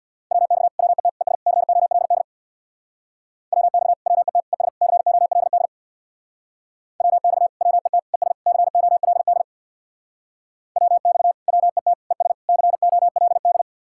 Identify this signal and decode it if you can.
Morse code, heard as a keyed tone, about 62 wpm, 700 Hz. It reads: WX WET ES COLD WX WET ES COLD WX WET ES COLD WX WET ES COLD
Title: Speed-Racing Common QSO Phrases - 50wpm